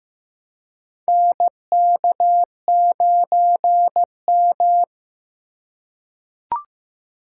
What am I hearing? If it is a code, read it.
Morse code, heard as a keyed tone, about 15 wpm, 700 Hz. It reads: NK9M
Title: USA Call Signs - 4x Character Spacing - 15wpm